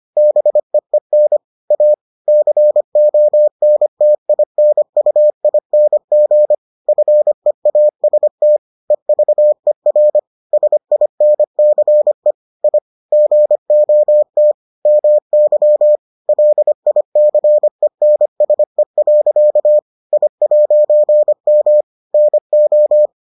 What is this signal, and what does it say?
Morse code, heard as a keyed tone, about 25 wpm, 600 Hz. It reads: BEEN A CONTINUING FEAST EVER SINCE I GOT MY LICENSE. I'M NO